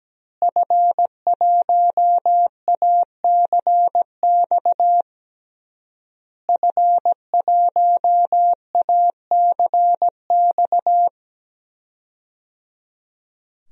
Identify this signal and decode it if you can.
Morse code, heard as a keyed tone, about 17 wpm, 700 Hz. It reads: F1ACX F1ACX
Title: DXCC-100 Call Signs - 5 Characters - Repeated 3x - 17wpm